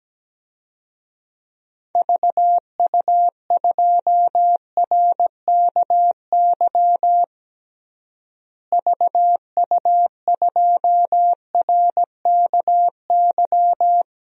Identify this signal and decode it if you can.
Morse code, heard as a keyed tone, about 17 wpm, 700 Hz. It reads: VU2RKY VU2RKY
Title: DXCC-100 Call Signs - 6 Characters - Repeated 3x - 17wpm